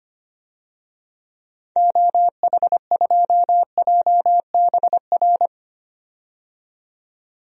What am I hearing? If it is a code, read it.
Morse code, heard as a keyed tone, about 25 wpm, 690 Hz. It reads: OH2JBR
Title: DXCC-100 Call Signs - 6 Characters - 25wpm